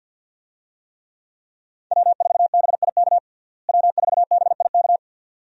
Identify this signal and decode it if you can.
Morse code, heard as a keyed tone, about 52 wpm, 700 Hz. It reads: W4BIK W4BIK
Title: USA Call Signs - 5 Characters - Repeated 2x - 50wpm